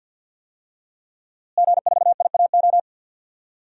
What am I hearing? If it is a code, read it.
Morse code, heard as a keyed tone, about 45 wpm, 700 Hz. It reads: G4IAO
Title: DXCC-100 Call Signs - 5 Characters - Repeated 2x - 50wpm